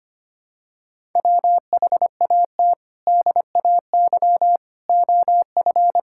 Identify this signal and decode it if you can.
Morse code, heard as a keyed tone, about 25 wpm, 700 Hz. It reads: WHAT DAY OF
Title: Sentences from Top 1000 Words - Review Part 5 - 25wpm